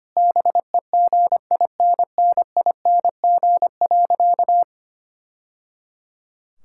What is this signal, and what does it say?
Morse code, heard as a keyed tone, about 25 wpm, 700 Hz. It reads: BEGINNING.
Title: Sentences from Top 1000 Words - Review Part 5 - 25wpm